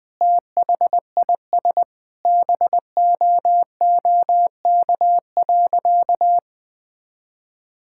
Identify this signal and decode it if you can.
Morse code, heard as a keyed tone, about 20 wpm, 700 Hz. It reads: THIS BOOK.
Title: Sentences with 2-Word Phrases - 20wpm